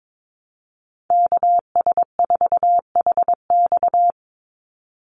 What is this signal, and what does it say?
Morse code, heard as a keyed tone, about 22 wpm, 700 Hz. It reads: KS4HX
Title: USA Call Signs 22wpm